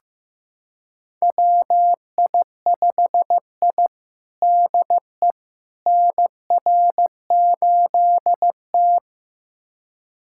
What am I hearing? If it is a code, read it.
Morse code, heard as a keyed tone, about 15 wpm, 700 Hz. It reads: WI5I DE NR8T
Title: USA Callsign DE Callsign - 4 Characters - 15wpm